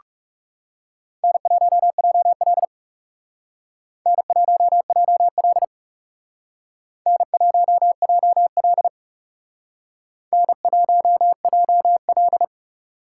Morse code, heard as a keyed tone, about 38 wpm, 700 Hz. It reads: N1JL N1JL N1JL N1JL